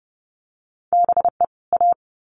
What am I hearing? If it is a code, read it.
Morse code, heard as a keyed tone, about 30 wpm, 700 Hz. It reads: BE A
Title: Sets of 2 Words 30wpm